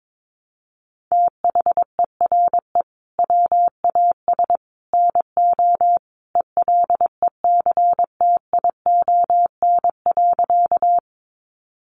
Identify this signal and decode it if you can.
Morse code, heard as a keyed tone, about 22 wpm, 700 Hz. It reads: THERE WAS NO ELECTION.
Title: Sentences from Top 1500 Words 22wpm